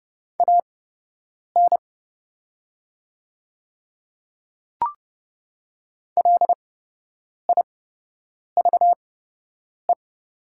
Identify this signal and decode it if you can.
Morse code, heard as a keyed tone, about 30 wpm, spaced as for 8 wpm, 700 Hz. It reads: AN LIVE
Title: 8x Character Spacing - Top 300 Words - 30wpm